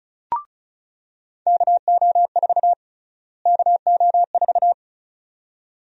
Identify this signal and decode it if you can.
Morse code, heard as a keyed tone, about 36 wpm, 700 Hz. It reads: KO4 KO4
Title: USA Call Sign Prefixes - Repeated 2x - 35wpm